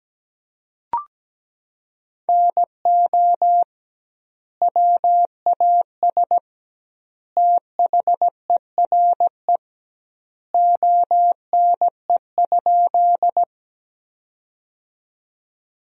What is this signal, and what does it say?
Morse code, heard as a keyed tone, about 17 wpm, 700 Hz. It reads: NO WAS THERE ONE?
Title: 2x Word-Spacing - Sentences from Top 100 Words - 17wpm